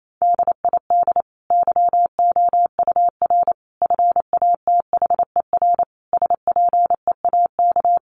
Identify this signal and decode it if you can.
Morse code, heard as a keyed tone, about 28 wpm, 700 Hz. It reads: DID YOUR FATHER SPEAK
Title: Sentences from Top 1100 Words 28wpm